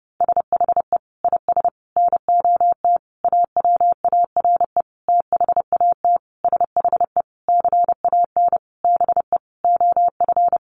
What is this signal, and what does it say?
Morse code, heard as a keyed tone, about 30 wpm, 700 Hz. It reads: SHE IS NOT AWARE THAT SHE CAN BE OF
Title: Sentences from Top 1200 Words 30wpm